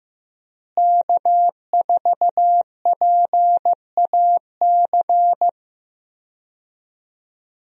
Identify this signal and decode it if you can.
Morse code, heard as a keyed tone, about 15 wpm, 700 Hz. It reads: K4PAC
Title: USA Call Signs - 5 Characters - Repeated 3x - 15wpm